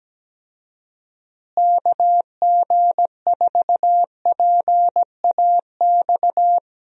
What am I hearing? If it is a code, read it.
Morse code, heard as a keyed tone, about 17 wpm, 700 Hz. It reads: KG4PAX
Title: USA Call Signs - 6 Characters - Repeated 2x - 17wpm